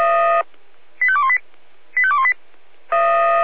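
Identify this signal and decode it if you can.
Five-tone selective call (ZVEI), heard as quick runs of five tones, tones of about 70 ms. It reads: 85218 85218
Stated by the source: ZVEI five-tone selective call received by SDR.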